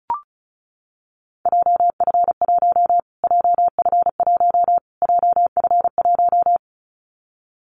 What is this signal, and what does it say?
Morse code, heard as a keyed tone, about 34 wpm, 700 Hz. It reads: JF1 JF1 JF1
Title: Callsign Prefixes for Top 25 DXCC - Repeated 3x - 35wpm